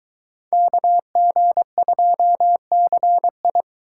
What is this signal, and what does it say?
Morse code, heard as a keyed tone, about 23 wpm, 700 Hz. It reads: KG2CI